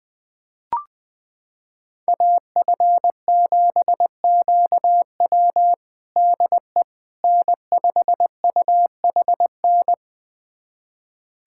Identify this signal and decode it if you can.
Morse code, heard as a keyed tone, about 20 wpm, 700 Hz. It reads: AF7QW DE N5UHN
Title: USA Callsign DE Callsign - 5 Characters - 20wpm